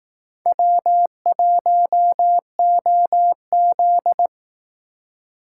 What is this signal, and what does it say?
Morse code, heard as a keyed tone, about 18 wpm, 700 Hz. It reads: W1OZ
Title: Speed Racing - USA 1x2 Call Signs 15wpm